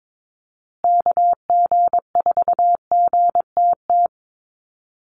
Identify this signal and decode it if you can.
Morse code, heard as a keyed tone, about 22 wpm, 700 Hz. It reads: KG4GTT
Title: USA Call Signs Repeated 2x 22wpm